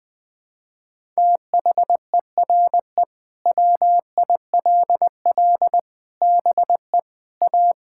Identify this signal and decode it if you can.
Morse code, heard as a keyed tone, about 20 wpm, 700 Hz. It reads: THERE WILL BE A
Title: Sentences with 3-Word Phrases - 20wpm